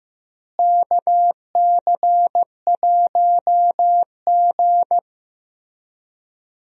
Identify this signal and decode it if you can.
Morse code, heard as a keyed tone, about 15 wpm, 700 Hz. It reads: KC1G
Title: USA Call Signs - 4 Characters - Repeated 3x - 15wpm